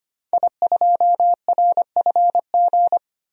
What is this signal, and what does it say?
Morse code, heard as a keyed tone, about 25 wpm, 700 Hz. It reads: I2RFG
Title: DXCC-100 Call Signs - 5 Characters - 25wpm